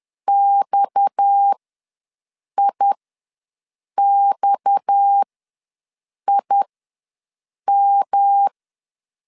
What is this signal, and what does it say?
Morse code, heard as a keyed tone, about 10 wpm, spaced as for 6 wpm, 800 Hz. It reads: XIXIM
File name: LEZIONI / 6 PKX0 / 6 PKX ws 10wpm 3706